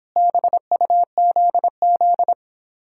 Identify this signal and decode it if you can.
Morse code, heard as a keyed tone, about 26 wpm, 700 Hz. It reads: BUZZ